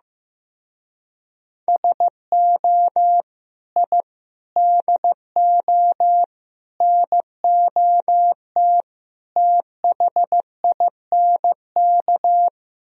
Morse code, heard as a keyed tone, about 15 wpm, 700 Hz. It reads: SO I DO NOT THINK
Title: Speed-Racing - Sets of 5 Words 15wpm